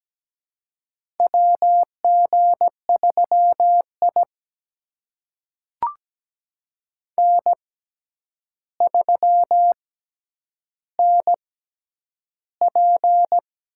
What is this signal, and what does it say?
Morse code, heard as a keyed tone, about 17 wpm, spaced as for 9 wpm, 700 Hz. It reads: WG3I N3NP
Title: USA Call Signs - 4 Characters - 6x Character Spacing - 17wpm